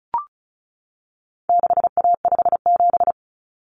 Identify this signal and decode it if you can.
Morse code, heard as a keyed tone, about 34 wpm, 700 Hz. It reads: 6A57
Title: Common Cut Numbers - 4-Digit 35wpm